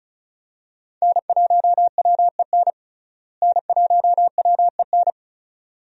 Morse code, heard as a keyed tone, about 36 wpm, 700 Hz. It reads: N1WEN N1WEN
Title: USA Call Signs - 5 Characters - Repeated 2x - 35wpm